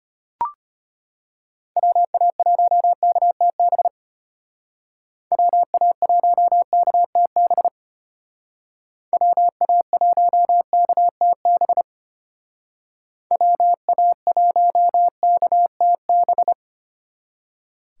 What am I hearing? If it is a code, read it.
Morse code, heard as a keyed tone, about 32 wpm, 700 Hz. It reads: WA1KTB WA1KTB WA1KTB WA1KTB